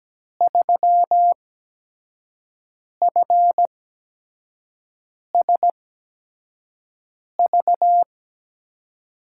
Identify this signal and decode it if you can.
Morse code, heard as a keyed tone, about 17 wpm, spaced as for 5 wpm, 700 Hz. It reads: 3FSV
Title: USA Call Signs - 5 Characters - 8x Character Spacing - 17wpm